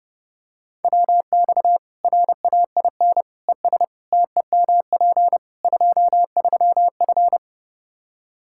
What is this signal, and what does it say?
Morse code, heard as a keyed tone, about 30 wpm, 700 Hz. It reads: WX RAIN ES TEMP 23F
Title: Practice QSO - Exchange 2 - Info - 30wpm